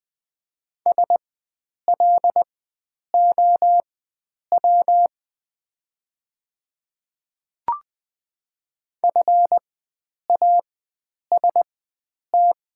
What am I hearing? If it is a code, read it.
Morse code, heard as a keyed tone, about 20 wpm, spaced as for 9 wpm, 700 Hz. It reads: SLOW FAST
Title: 4x Character Spacing - Top 500 Words - 20wpm